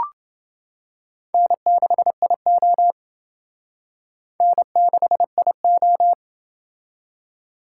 Morse code, heard as a keyed tone, about 28 wpm, 700 Hz. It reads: N6IO N6IO